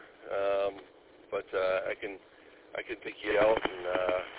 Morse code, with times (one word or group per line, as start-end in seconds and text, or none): none